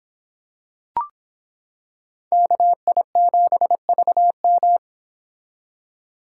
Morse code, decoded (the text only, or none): KI7VM